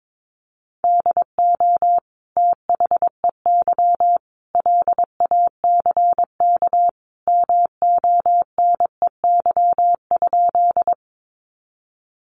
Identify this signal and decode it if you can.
Morse code, heard as a keyed tone, about 22 wpm, 700 Hz. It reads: DO THEY LACK MONEY?
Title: Sentences from Top 1100 Words 22wpm